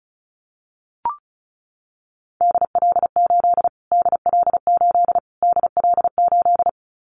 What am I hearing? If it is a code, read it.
Morse code, heard as a keyed tone, about 34 wpm, 700 Hz. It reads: DL8 DL8 DL8
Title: Callsign Prefixes for Top 25 DXCC - Repeated 3x - 35wpm